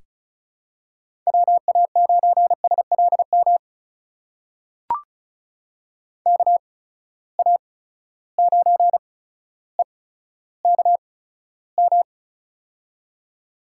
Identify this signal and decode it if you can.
Morse code, heard as a keyed tone, about 36 wpm, spaced as for 15 wpm, 700 Hz. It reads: WA9SLM KA9EKM
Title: USA Call Signs - 6 Characters - 8x Character Spacing - 35wpm